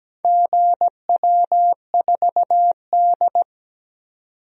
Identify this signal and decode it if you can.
Morse code, heard as a keyed tone, about 17 wpm, 700 Hz. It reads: GW4D
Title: DXCC-100 Call Signs - 4 Characters - 17wpm